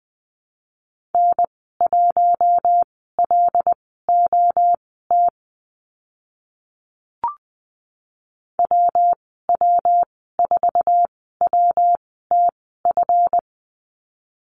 Morse code, heard as a keyed tone, about 20 wpm, spaced as for 14 wpm, 700 Hz. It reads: N1LOT WW4WTF
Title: 186 Lesson 30g - Letter b - Call Signs 20wpm